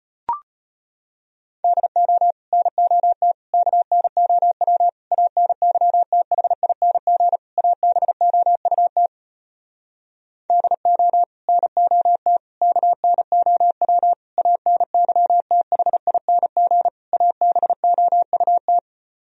Morse code, heard as a keyed tone, about 34 wpm, 700 Hz. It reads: DO NOT KNOW ANYTHING ABOUT DO NOT KNOW ANYTHING ABOUT